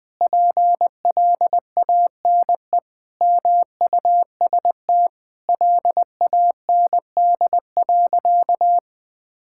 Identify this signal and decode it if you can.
Morse code, heard as a keyed tone, about 20 wpm, 700 Hz. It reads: PLANE MUST LAND.
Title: Sentences from Top 1000 Words - Review Part 6 - 20wpm